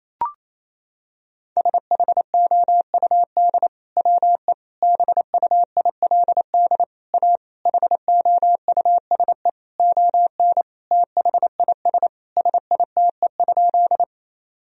Morse code, heard as a keyed tone, about 28 wpm, 700 Hz. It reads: SHOUD WE BUILD A HOUSE ON THIS SITE?
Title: Sentences from Top 1000 Words - Review Part 4 - 28wpm